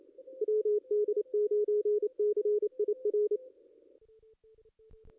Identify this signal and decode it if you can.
Morse code, heard as a keyed tone, about 28 wpm, 440 Hz, with radio noise. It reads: WD9CIR
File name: contests / ss-cw-k5tr-op-n5rz -2200